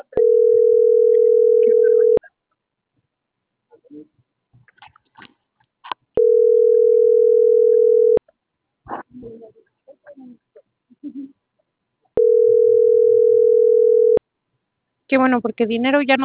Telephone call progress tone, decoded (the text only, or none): ringback tone